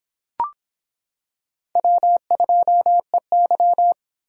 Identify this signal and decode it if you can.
Morse code, heard as a keyed tone, about 26 wpm, 700 Hz. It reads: W2EY